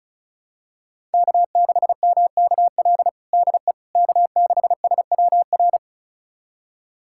Morse code, heard as a keyed tone, about 36 wpm, 700 Hz. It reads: K6MKL DE K6SWR
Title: USA Callsign DE Callsign - 5 Characters - 35wpm